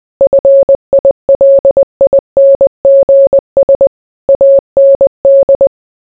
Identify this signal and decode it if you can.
Morse code, heard as a keyed tone, about 20 wpm, 550 Hz. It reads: FILINGS AND